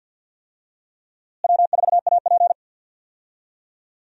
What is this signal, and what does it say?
Morse code, heard as a keyed tone, about 46 wpm, 700 Hz. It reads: W4AP
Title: USA Call Signs - 4 Characters - Repeated 3x - 50wpm